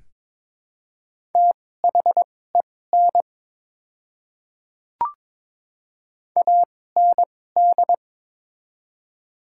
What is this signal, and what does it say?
Morse code, heard as a keyed tone, about 22 wpm, spaced as for 16 wpm, 700 Hz. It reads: THEN AND
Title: 2x Character Spacing - Top 100 Words - 22wpm